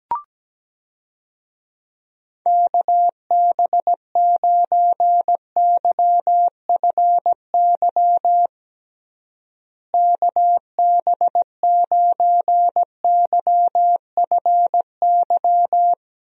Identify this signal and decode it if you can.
Morse code, heard as a keyed tone, about 17 wpm, 700 Hz. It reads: KB9YFY KB9YFY
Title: USA Call Signs - 6 Characters - Repeated 3x - 17wpm